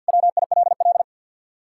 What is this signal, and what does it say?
Morse code, heard as a keyed tone, about 66 wpm, spaced as for 58 wpm, 700 Hz. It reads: WILL
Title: Speed-Racing Common QSO Phrases - 50wpm